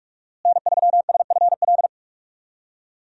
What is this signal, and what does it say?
Morse code, heard as a keyed tone, about 46 wpm, 700 Hz. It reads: N3SFL
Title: USA Call Signs - 5 Characters - Repeated 3x - 45wpm